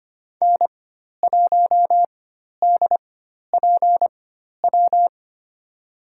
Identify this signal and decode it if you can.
Morse code, heard as a keyed tone, about 25 wpm, spaced as for 12 wpm, 700 Hz. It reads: N1DPW
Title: USA Call Signs - 5 Characters - 4x Character Spacing - 25wpm